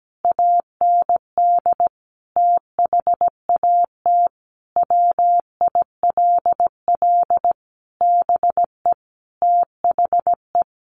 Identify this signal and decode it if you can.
Morse code, heard as a keyed tone, about 17 wpm, 700 Hz. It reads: AND THAT WILL BE THE
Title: Sentences from Top 1200 Words 17wpm